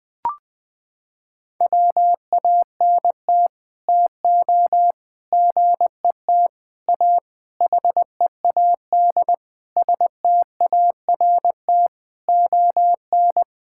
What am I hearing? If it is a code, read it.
Morse code, heard as a keyed tone, about 21 wpm, 700 Hz. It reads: WANT TO GET A HEAD START ON
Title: Sets of 9 Words - 20wpm